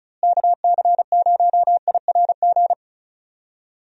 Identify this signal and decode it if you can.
Morse code, heard as a keyed tone, about 36 wpm, 700 Hz. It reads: KC0IRG
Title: USA Call Signs - 6 Characters - Repeated 2x - 35wpm